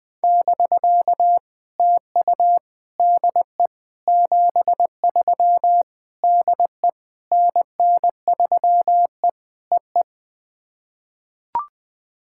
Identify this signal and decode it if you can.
Morse code, heard as a keyed tone, about 20 wpm, 690 Hz. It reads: <BK> TU DE 73 DE NN3E EE
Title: POTA QSO Practice - 20wpm